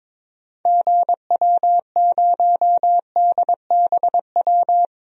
Morse code, heard as a keyed tone, about 21 wpm, 700 Hz. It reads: GW0DBW
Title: DXCC-100 Call Signs - 6 Characters - 22wpm